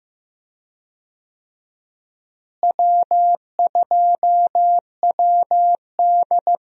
Morse code, heard as a keyed tone, about 15 wpm, 700 Hz. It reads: W2WD
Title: USA Call Signs - 4 Characters - Repeated 2x - 15wpm